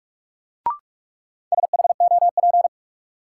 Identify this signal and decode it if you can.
Morse code, heard as a keyed tone, about 44 wpm, 700 Hz. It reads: SHOP
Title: Sentences from Top 1600 Words - New Word Drill - 45wpm